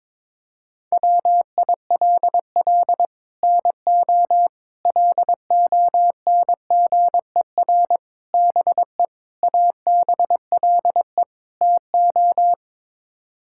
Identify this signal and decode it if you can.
Morse code, heard as a keyed tone, about 22 wpm, 700 Hz. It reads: WILL NO LONGER BE ABLE TO
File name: practice / 22wpm / Sets of 6 Words - 22wpm